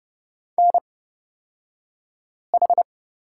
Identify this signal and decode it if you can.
Morse code, heard as a keyed tone, about 30 wpm, spaced as for 5 wpm, 700 Hz. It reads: NH
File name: Lesson 36d - BT - Two Characters 30wpm